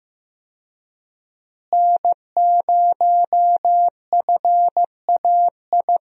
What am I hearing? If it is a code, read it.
Morse code, heard as a keyed tone, about 15 wpm, 700 Hz. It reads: N0FAI
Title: USA Call Signs - 5 Characters - Repeated 3x - 15wpm